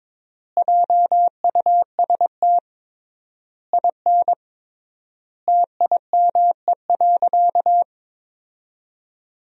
Morse code, heard as a keyed tone, about 22 wpm, 700 Hz. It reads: JUST IN TIME.
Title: 3x Word-Spacing - Sentences from Top 300 Words - 22wpm